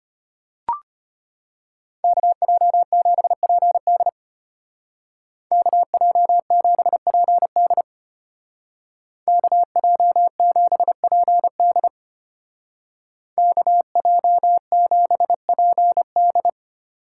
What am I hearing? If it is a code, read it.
Morse code, heard as a keyed tone, about 30 wpm, 700 Hz. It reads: KJ7PD KJ7PD KJ7PD KJ7PD
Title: Speed Racing - USA 2x2 Call Signs 25wpm